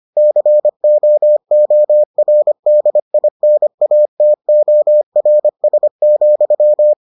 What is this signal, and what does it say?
Morse code, heard as a keyed tone, about 25 wpm, 600 Hz. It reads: COORDINATORS,